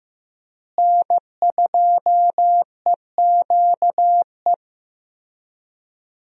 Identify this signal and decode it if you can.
Morse code, heard as a keyed tone, about 15 wpm, 700 Hz. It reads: N2EQE
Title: USA Call Signs - 5 Characters - Repeated 2x - 15wpm